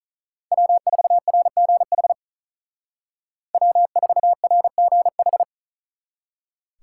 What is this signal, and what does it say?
Morse code, heard as a keyed tone, about 46 wpm, 700 Hz. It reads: W4RGH W4RGH